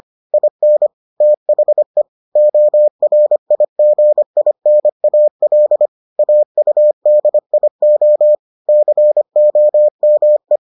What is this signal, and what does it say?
Morse code, heard as a keyed tone, about 25 wpm, 600 Hz. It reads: IN THE ORIGINAL AUDIO COME